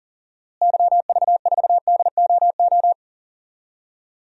Morse code, heard as a keyed tone, about 40 wpm, 700 Hz. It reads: YV4DOO
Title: DXCC-100 Call Signs - 6 Characters - Repeated 3x - 40wpm